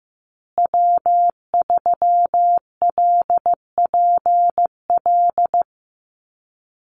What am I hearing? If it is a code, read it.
Morse code, heard as a keyed tone, about 15 wpm, 700 Hz. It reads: W3LPL
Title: Call Signs Repeated 3x - 15wpm